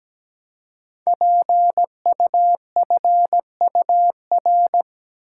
Morse code, heard as a keyed tone, about 17 wpm, 700 Hz. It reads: PUFUR
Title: Code Groups - 5 Letters - 17wpm